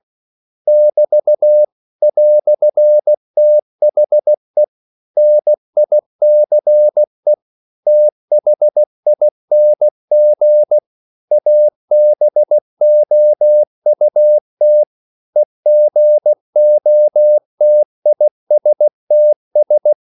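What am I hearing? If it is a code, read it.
Morse code, heard as a keyed tone, about 16 wpm, 600 Hz. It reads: <BT> "THE NICE THING ABOUT EGOTISTS
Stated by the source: English quotations sent in CW as practice at 16 words per minute.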